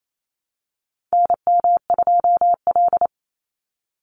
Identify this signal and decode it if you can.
Morse code, heard as a keyed tone, about 28 wpm, 700 Hz. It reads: NM2L
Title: Call Signs Repeated 3x - 28wpm